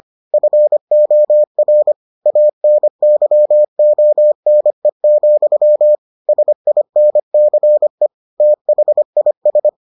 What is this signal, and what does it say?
Morse code, heard as a keyed tone, about 25 wpm, 600 Hz. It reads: FOR ANYONE, SINCE THIS